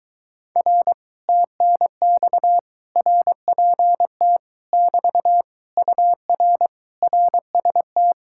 Morse code, heard as a keyed tone, about 22 wpm, 700 Hz. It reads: R TNX RPT <BT> UR RST